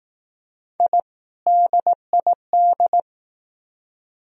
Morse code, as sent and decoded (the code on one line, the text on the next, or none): .. / -.. .. -..
I DID